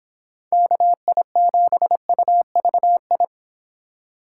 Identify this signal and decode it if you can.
Morse code, heard as a keyed tone, about 21 wpm, 700 Hz. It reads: KI7UVI
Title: Speed Racing - USA Call Signs 17wpm